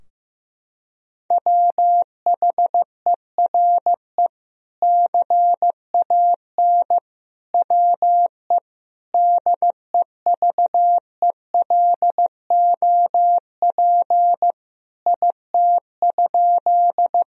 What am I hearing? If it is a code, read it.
Morse code, heard as a keyed tone, about 15 wpm, 700 Hz. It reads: WHERE CAN WE DEVELOP IT?
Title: Sentences from Top 1000 Words - Review Part 2 - 15wpm